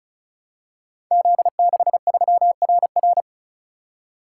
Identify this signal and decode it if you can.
Morse code, heard as a keyed tone, about 36 wpm, 700 Hz. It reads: Z63RR
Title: Code Groups - 5 Letter:Numbers - 35wpm